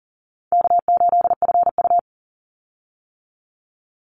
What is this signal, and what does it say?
Morse code, heard as a keyed tone, about 40 wpm, 700 Hz. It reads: K8FU
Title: Easy Call Signs - 40wpm